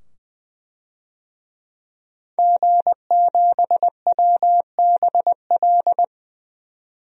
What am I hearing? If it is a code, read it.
Morse code, heard as a keyed tone, about 20 wpm, 700 Hz. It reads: G7WBL
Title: DXCC-100 Call Signs - 5 Characters - 20wpm